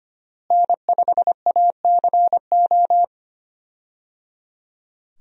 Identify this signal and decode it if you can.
Morse code, heard as a keyed tone, about 25 wpm, 700 Hz. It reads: N5ACO
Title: USA Call Signs - 5 Characters - Repeated 3x - 25wpm